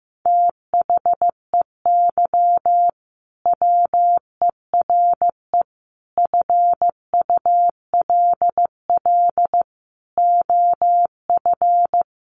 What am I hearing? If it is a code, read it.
Morse code, heard as a keyed tone, about 15 wpm, 700 Hz. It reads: THEY WERE FULL OF
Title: Sentences from Top 1400 Words 15wpm